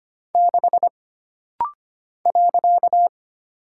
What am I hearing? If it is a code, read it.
Morse code, heard as a keyed tone, about 25 wpm, 700 Hz. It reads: 6 .
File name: Single Character (Letter:Number:Punctuation:Prosign) - Rapid-Fire - 25wpm